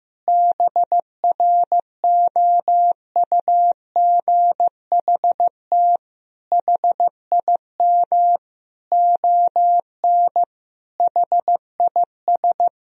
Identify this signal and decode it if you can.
Morse code, heard as a keyed tone, about 16 wpm, 700 Hz. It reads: BROUGHT HIM ON HIS
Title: Sentences from Top 1000 Words - Review Part 2 - 15wpm